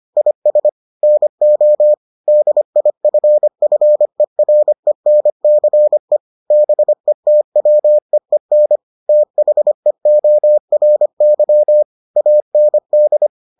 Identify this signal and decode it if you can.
Morse code, heard as a keyed tone, about 25 wpm, 600 Hz. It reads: IS NO DIFFERENCE BETWEEN THEORY AND